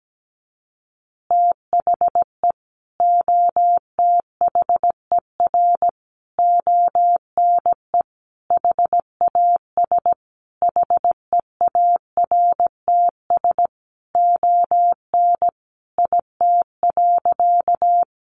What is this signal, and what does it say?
Morse code, heard as a keyed tone, about 17 wpm, 700 Hz. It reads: THE OTHER ONE HAS HEARTS ON IT.
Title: Additional Easy Phrases 17wpm